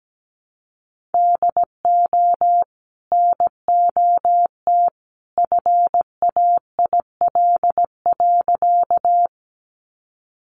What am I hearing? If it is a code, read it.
Morse code, heard as a keyed tone, about 17 wpm, 700 Hz. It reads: DO NOT FAIL.
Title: Sentences from Top 1100 Words 17wpm